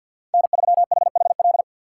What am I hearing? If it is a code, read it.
Morse code, heard as a keyed tone, about 64 wpm, 700 Hz. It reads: N3HHL